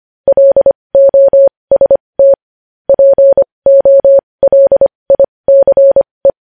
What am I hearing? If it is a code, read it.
Morse code, heard as a keyed tone, about 25 wpm, spaced as for 20 wpm, 550 Hz. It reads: LOST POLICE
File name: podcast / News headlines-1x-25x20